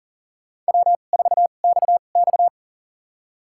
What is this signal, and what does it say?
Morse code, heard as a keyed tone, about 38 wpm, spaced as for 28 wpm, 690 Hz. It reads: W4XX
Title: USA Call Signs - 4 Characters - 2x Character Spacing - 40wpm